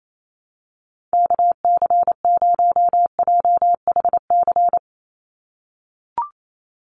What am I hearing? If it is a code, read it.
Morse code, heard as a keyed tone, about 28 wpm, 700 Hz. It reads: KC0JHC
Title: USA Call Signs 28wpm